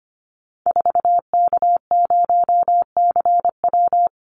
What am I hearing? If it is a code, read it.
Morse code, heard as a keyed tone, about 25 wpm, 700 Hz. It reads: 4K0CW
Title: Call Signs Repeated 3x - 25wpm